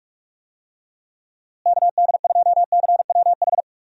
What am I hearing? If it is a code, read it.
Morse code, heard as a keyed tone, about 46 wpm, 700 Hz. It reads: KD2CWH
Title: USA Call Signs - 6 Characters - Repeated 3x - 45wpm